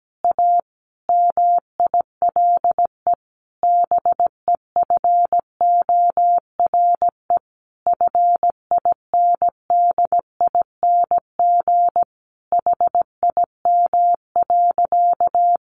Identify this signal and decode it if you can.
Morse code, heard as a keyed tone, about 17 wpm, 700 Hz. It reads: A MILE BEFORE FINDING HIM.
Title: Sentences from Top 1100 Words 17wpm